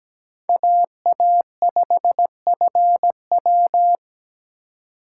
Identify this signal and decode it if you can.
Morse code, heard as a keyed tone, about 20 wpm, 700 Hz. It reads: AA5FW
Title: Speed Racing - USA 2x2 Call Signs 17wpm